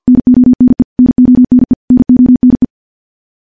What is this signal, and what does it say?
Morse code, heard as a keyed tone, about 50 wpm, 250 Hz. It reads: NONE NONE NONE
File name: morse / morse-training-50WPM